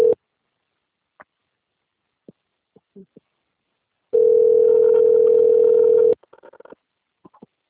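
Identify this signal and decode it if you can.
Ringback tone (North America), 440 + 480 Hz.
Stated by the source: Outgoing phone call recorded on the Atibaia - SP 1 (CP Cartao) system.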